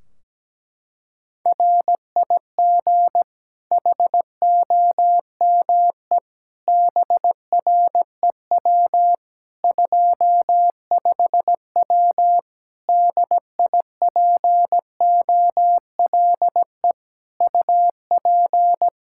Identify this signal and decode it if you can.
Morse code, heard as a keyed tone, about 17 wpm, 700 Hz. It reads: RIG HOME BREW 25W DIPOLE UP
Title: K1USN SST QSO Practice 17wpm